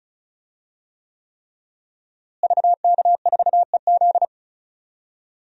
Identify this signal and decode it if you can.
Morse code, heard as a keyed tone, about 36 wpm, 700 Hz. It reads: VK4EZ
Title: DXCC-100 Call Signs - 5 Characters - Repeated 3x - 35wpm